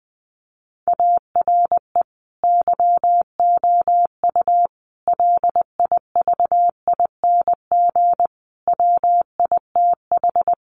Morse code, heard as a keyed tone, about 20 wpm, 700 Hz. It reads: ARE YOU LIVING WITH
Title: Sentences from Top 1100 Words 20wpm